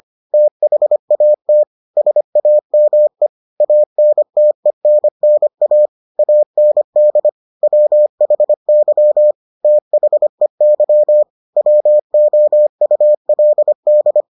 Morse code, heard as a keyed tone, about 25 wpm, 600 Hz. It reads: THAT SAME ANTENNA AND WHY THEY WOULD